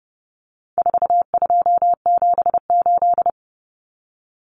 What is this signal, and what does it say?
Morse code, heard as a keyed tone, about 30 wpm, 700 Hz. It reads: 4278
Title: Numbers - 4-Digit 30wpm